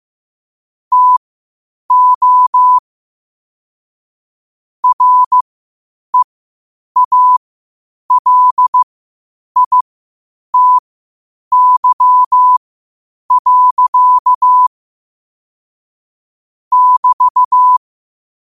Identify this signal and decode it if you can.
Morse code, heard as a keyed tone, about 15 wpm, spaced as for 8 wpm, 1000 Hz. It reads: TO REALITY. <BT>
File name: qotdfiles / 8wpm